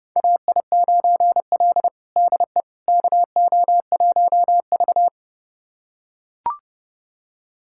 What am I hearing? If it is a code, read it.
Morse code, heard as a keyed tone, about 30 wpm, 700 Hz. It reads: AI9L DE KO1V
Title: USA Callsign DE Callsign - 4 Characters - 30wpm